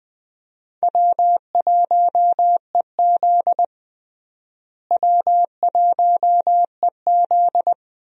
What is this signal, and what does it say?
Morse code, heard as a keyed tone, about 20 wpm, 700 Hz. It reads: W1EZ W1EZ
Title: USA Call Signs - 4 Characters - Repeated 2x - 20wpm